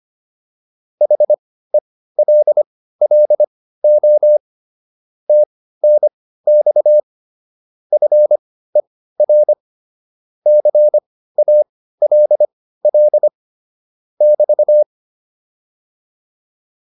Morse code, heard as a keyed tone, about 25 wpm, spaced as for 15 wpm, 600 Hz. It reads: HELLO TNX FER CALL <BT>